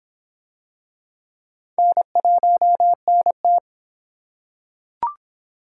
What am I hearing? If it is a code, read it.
Morse code, heard as a keyed tone, about 23 wpm, 700 Hz. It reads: N1NT